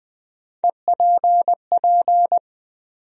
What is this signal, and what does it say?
Morse code, heard as a keyed tone, about 20 wpm, 700 Hz. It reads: EPP
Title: USA Call Sign Postfixes - 3 Letters 20wpm